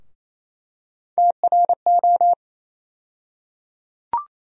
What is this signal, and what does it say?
Morse code, heard as a keyed tone, about 28 wpm, 700 Hz. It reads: TRO
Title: USA Call Sign Postfixes - 3 Letters 28wpm